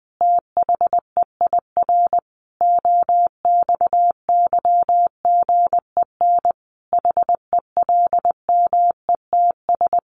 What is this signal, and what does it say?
Morse code, heard as a keyed tone, about 19 wpm, 700 Hz. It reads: THEIR OXYGEN HELMETS
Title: A Sound Of Thunder 20wpm